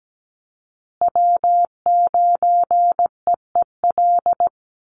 Morse code, heard as a keyed tone, about 17 wpm, 700 Hz. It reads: W9EEL
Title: USA Call Signs Repeated 2x 17wpm